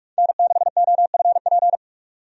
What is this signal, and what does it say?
Morse code, heard as a keyed tone, about 46 wpm, 700 Hz. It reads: N6OFP
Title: USA Call Signs - 5 Characters - 45wpm